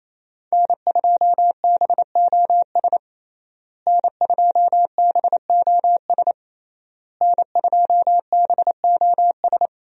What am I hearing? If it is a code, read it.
Morse code, heard as a keyed tone, about 28 wpm, 700 Hz. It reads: N2BOS N2BOS N2BOS
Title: USA Call Signs - 5 Characters - Repeated 3x - 28wpm